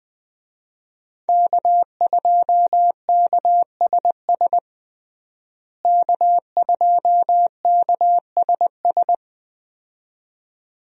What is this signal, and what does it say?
Morse code, heard as a keyed tone, about 21 wpm, 700 Hz. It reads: K2KSS K2KSS
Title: USA Call Signs - 5 Characters - Repeated 2x - 20wpm